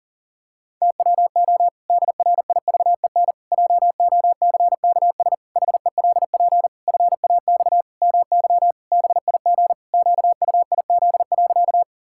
Morse code, heard as a keyed tone, about 40 wpm, 700 Hz. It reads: TWO DRIVEN JOCKS HELP FAX MY BIG QUIZ.
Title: Pangram Challenge v2 - 40wpm